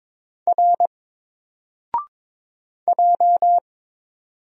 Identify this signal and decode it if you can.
Morse code, heard as a keyed tone, about 22 wpm, 700 Hz. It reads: R J
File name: Single-Letter-Number-22wpm